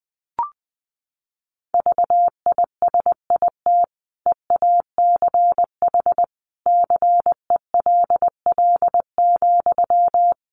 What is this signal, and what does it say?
Morse code, heard as a keyed tone, about 20 wpm, 700 Hz. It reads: VISIT EACH CELL,
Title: Sentences from Top 1100 Words 20wpm